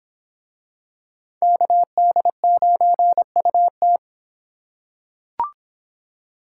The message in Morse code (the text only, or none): KD9UT